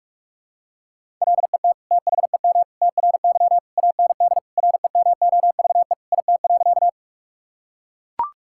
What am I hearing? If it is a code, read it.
Morse code, heard as a keyed tone, about 42 wpm, 700 Hz. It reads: LET THEM TRY AND REMOVE IT.
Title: Sentences from Top 1000 Words - Review Part 5 - 45wpm